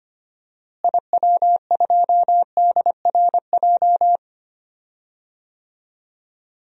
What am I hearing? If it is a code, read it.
Morse code, heard as a keyed tone, about 25 wpm, 700 Hz. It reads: IW2DRJ
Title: DXCC-100 Call Signs - 6 Characters - Repeated 3x - 25wpm